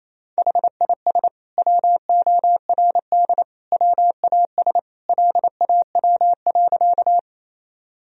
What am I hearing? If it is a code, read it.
Morse code, heard as a keyed tone, about 28 wpm, 700 Hz. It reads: HIS WORD WAS LAW.
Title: Sentences from Top 1000 Words - Review Part 3 - 28wpm